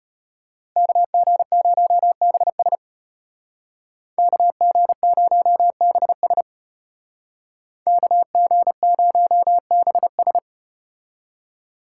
Morse code, heard as a keyed tone, about 34 wpm, 700 Hz. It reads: KG0BS KG0BS KG0BS